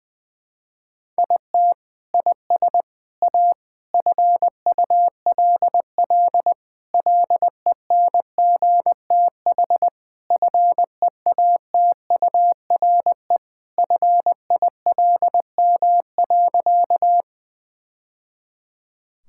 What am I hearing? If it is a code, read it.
Morse code, heard as a keyed tone, about 20 wpm, 700 Hz. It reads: IT IS A FULL LENGTH FEATURE FILM.
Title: Sentences from Top 1000 Words - Review Part 5 - 20wpm